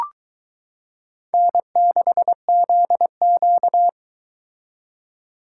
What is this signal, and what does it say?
Morse code, heard as a keyed tone, about 22 wpm, 700 Hz. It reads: N6ZQ